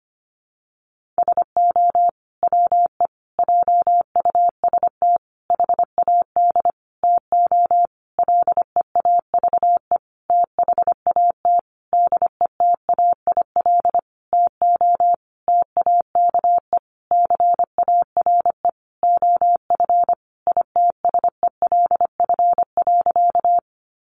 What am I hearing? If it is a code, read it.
Morse code, heard as a keyed tone, about 25 wpm, 700 Hz. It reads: SO WE JUST HAD TO LEAVE THAT DETAIL TO TAKE CARE OF ITSELF.
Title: Sentences from Top 1500 Words 25wpm